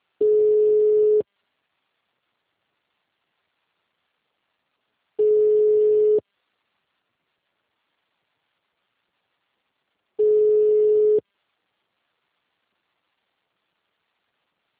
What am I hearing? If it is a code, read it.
Ringback tone (Europe, 425 Hz).
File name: out- -2001- - - .12541